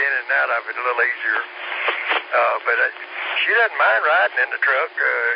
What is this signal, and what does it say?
Emergency Alert System (SAME) tones, heard as harsh buzzing bursts, with speech. It is heard from 1.0 to 4.6 s.